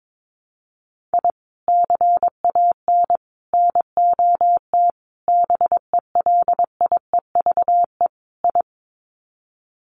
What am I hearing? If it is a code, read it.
Morse code, heard as a keyed tone, about 22 wpm, 700 Hz. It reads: I CAN NOT BELIEVE I
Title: Sets of 5 Words Encore2 22wpm